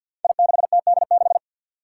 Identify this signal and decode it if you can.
Morse code, heard as a keyed tone, about 53 wpm, 700 Hz. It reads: I6TD6
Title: Five Challenging Letter-Number Combinations 50wpm